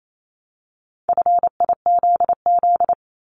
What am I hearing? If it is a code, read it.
Morse code, heard as a keyed tone, about 28 wpm, 700 Hz. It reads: FIZZ
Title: Multi-letter Phonemes 28wpm